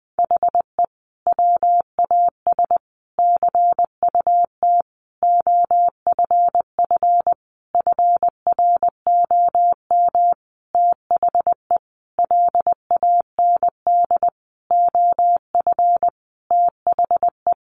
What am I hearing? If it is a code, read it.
Morse code, heard as a keyed tone, about 20 wpm, 700 Hz. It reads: HE WAS CUT OFF FROM THE LAND OF THE
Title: Sentences from Top 1100 Words 20wpm